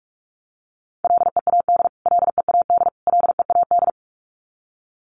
Morse code, heard as a keyed tone, about 44 wpm, 700 Hz. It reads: LEAD LEAD LEAD
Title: Top 500 Words Repeated 45wpm